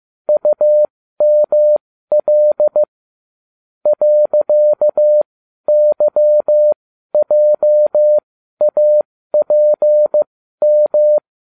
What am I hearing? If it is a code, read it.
Morse code, heard as a keyed tone, about 15 wpm, spaced as for 12 wpm, 600 Hz. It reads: UML .YJAPM